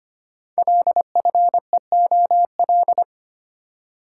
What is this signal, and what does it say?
Morse code, heard as a keyed tone, about 25 wpm, 700 Hz. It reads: LFEOL
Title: Code Groups - 5 Letters - 25wpm